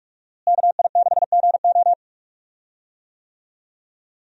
Morse code, heard as a keyed tone, about 43 wpm, 700 Hz. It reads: KI6GO
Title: USA Call Signs - 5 Characters - Repeated 2x - 45wpm